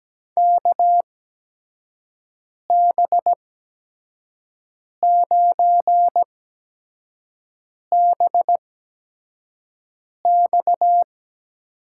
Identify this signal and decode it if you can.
Morse code, heard as a keyed tone, about 17 wpm, spaced as for 5 wpm, 700 Hz. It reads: KB9BX
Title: USA Call Signs - 6 Characters - 8x Character Spacing - 17wpm